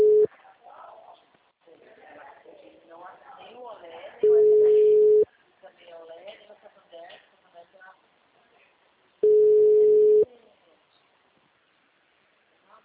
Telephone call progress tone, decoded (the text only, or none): ringback tone